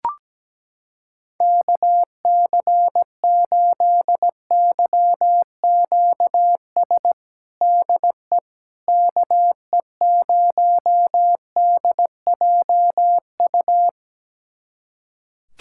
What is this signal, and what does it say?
Morse code, heard as a keyed tone, about 17 wpm, 700 Hz. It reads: KC8YQS DE KE0DJU
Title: USA Callsign DE Callsign 17wpm